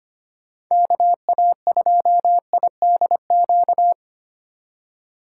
Morse code, heard as a keyed tone, about 25 wpm, 700 Hz. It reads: KA2IDQ
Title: USA Call Signs - 4x Character Spacing - 25wpm